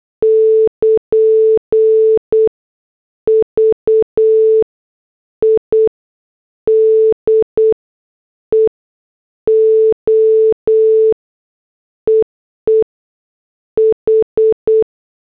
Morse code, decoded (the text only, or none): <KN> VIDEO EE H